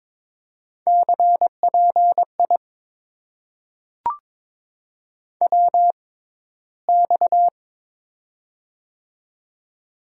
Morse code, heard as a keyed tone, about 22 wpm, spaced as for 14 wpm, 700 Hz. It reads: CPI WX
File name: Top 90 QSO Elements - 6x Character Spacing - 22wpm